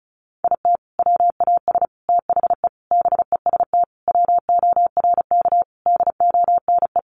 Morse code, heard as a keyed tone, about 34 wpm, 700 Hz. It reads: IT WAS THE BEST WORK DONE
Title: Sentences from Top 1300 Words 35wpm